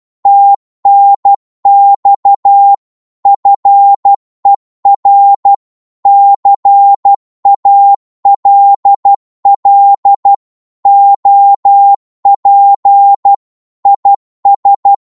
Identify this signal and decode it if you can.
Morse code, heard as a keyed tone, about 12 wpm, 800 Hz. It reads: TNX FER CALL OP IS
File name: Morse / qso-12wpm